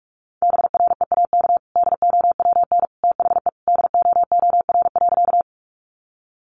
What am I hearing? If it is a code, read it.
Morse code, heard as a keyed tone, about 44 wpm, 700 Hz. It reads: BREAK DOWN THE DOOR.
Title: Sentences from Top 900 Words 45wpm